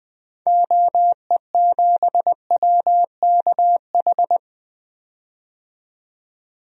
Morse code, heard as a keyed tone, about 20 wpm, 700 Hz. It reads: OE7WKH
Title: DXCC-100 Call Signs - 6 Characters - 20wpm